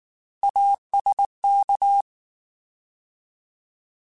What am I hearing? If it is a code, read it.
Morse code, heard as a keyed tone, about 19 wpm, 800 Hz. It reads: ASK
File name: Alphabetical / Alphabetical mp3s / As @18